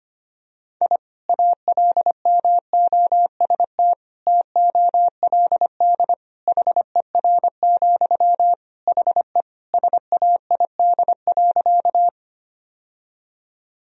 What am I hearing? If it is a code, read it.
Morse code, heard as a keyed tone, about 25 wpm, 700 Hz. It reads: I ALMOST TOLD HER, HE SAID.
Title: Sentences from Top 1000 Words - Review Part 4 - 25wpm